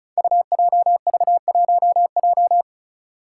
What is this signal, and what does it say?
Morse code, heard as a keyed tone, about 34 wpm, 700 Hz. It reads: UJV1J
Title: Five Challenging Letter-Number Combinations 35wpm